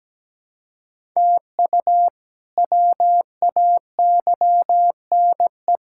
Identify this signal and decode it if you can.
Morse code, heard as a keyed tone, about 17 wpm, 700 Hz. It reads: TU WAYNE
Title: K1USN SST QSO Practice - 17wpm